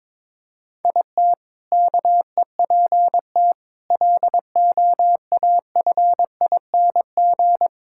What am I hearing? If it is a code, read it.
Morse code, heard as a keyed tone, about 21 wpm, 700 Hz. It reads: IT KEPT LOAFING
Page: morse-code-ninja-practice-22wpm